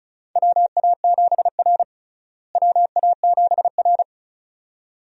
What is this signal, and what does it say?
Morse code, heard as a keyed tone, about 36 wpm, 700 Hz. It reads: WA7R WA7R
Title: USA Call Signs - 4 Characters - Repeated 2x - 35wpm